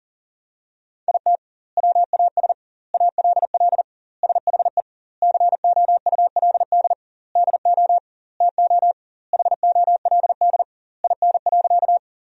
Morse code, heard as a keyed tone, about 40 wpm, 700 Hz. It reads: IT WAS ALL SHE COULD DO TO HOLD IN.
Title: 2x Word-Spacing - Sentences from Top 400 Words - 40wpm